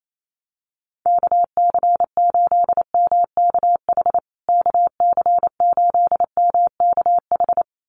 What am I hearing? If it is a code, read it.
Morse code, heard as a keyed tone, about 28 wpm, 700 Hz. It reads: KC8MKH KC8MKH
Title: USA Call Signs Repeated 2x 28wpm